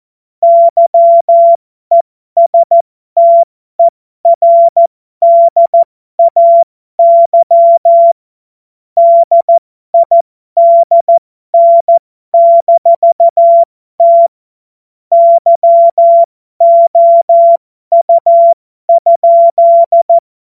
Morse code, heard as a keyed tone, about 14 wpm, spaced as for 12 wpm, 680 Hz. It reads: YESTERDAY DIDN-T YOU?
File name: Morsekurs / Witze